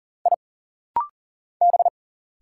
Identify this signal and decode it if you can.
Morse code, heard as a keyed tone, about 40 wpm, 700 Hz. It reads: I B
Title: Single Letters - Rapid-Fire 40wpm